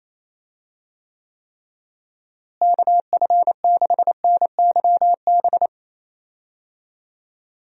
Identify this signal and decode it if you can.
Morse code, heard as a keyed tone, about 28 wpm, 700 Hz. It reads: KF6NYB
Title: USA Call Signs - 6 Characters - Repeated 3x - 28wpm